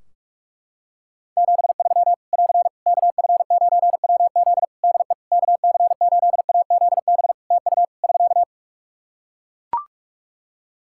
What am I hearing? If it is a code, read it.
Morse code, heard as a keyed tone, about 44 wpm, 700 Hz. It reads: 73 <AR> KF9WZ DE KC8AZB TU <SK>
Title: Practice QSO - Exchange 4 - Ending - 45wpm